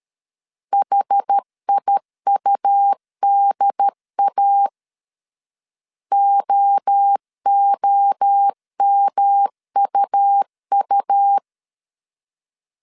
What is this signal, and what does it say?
Morse code, heard as a keyed tone, about 13 wpm, 800 Hz. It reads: HIUDA OOMUU
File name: CW / DET / 1 3 1500C S B 12wpm 2828